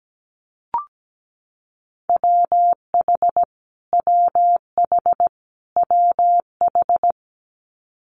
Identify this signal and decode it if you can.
Morse code, heard as a keyed tone, about 17 wpm, 700 Hz. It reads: WH WH WH
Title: Multi-letter Phonemes Repeated 3x 17wpm